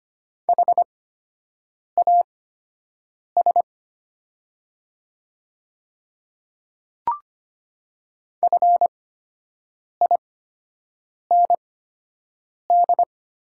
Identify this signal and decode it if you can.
Morse code, heard as a keyed tone, about 25 wpm, spaced as for 7 wpm, 700 Hz. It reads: HAS FIND
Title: 8x Character Spacing - Top 200 Words - 25wpm